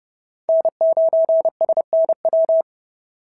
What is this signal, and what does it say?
Morse code, heard as a keyed tone, about 30 wpm, 690 Hz. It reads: N9SNW
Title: SOTA QSO Practice 30wpm